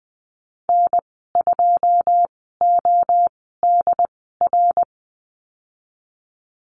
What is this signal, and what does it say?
Morse code, heard as a keyed tone, about 20 wpm, spaced as for 15 wpm, 700 Hz. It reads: N2ODR
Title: 082 Lesson 14g - Number 2 - Call Signs 20wpm